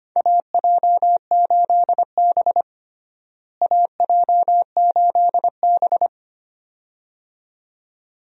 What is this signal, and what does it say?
Morse code, heard as a keyed tone, about 25 wpm, 700 Hz. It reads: AJ8B AJ8B
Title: USA Call Signs - 4 Characters - Repeated 2x - 25wpm